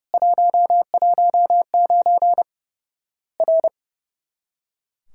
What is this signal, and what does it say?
Morse code, heard as a keyed tone, about 30 wpm, 700 Hz. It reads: 119 R
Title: Worked All Europe DX Contest - QTC Exchange - 30wpm